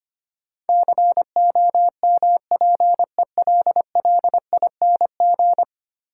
Morse code, heard as a keyed tone, about 25 wpm, 690 Hz. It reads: COMPELLING
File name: 10 Letter Words - First 3 Letters Given - 25wpm